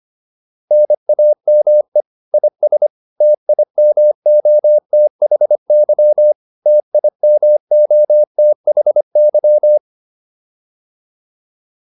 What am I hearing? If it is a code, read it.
Morse code, heard as a keyed tone, about 25 wpm, 600 Hz. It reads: NAME IS TIMOTHY TIMOTHY